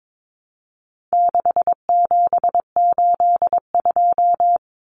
Morse code, heard as a keyed tone, about 22 wpm, 700 Hz. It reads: 6782
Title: Common Cut Numbers - 4-Digit 22wpm